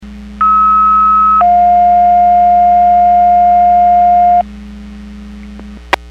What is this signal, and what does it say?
Two-tone page, heard as a one-second tone, then a three-second tone, about 1290 then 710 Hz. It is heard from 0.4 to 4.4 s.
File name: Portage 154.385 Paging